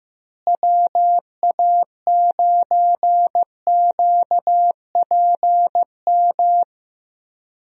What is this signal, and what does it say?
Morse code, heard as a keyed tone, about 15 wpm, 700 Hz. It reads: WA9QPM
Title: USA Call Signs - 6 Characters - Repeated 3x - 15wpm